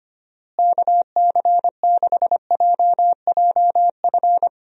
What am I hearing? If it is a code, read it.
Morse code, heard as a keyed tone, about 25 wpm, 700 Hz. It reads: KC6JJF
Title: USA Call Signs - 6 Characters - Repeated 2x - 25wpm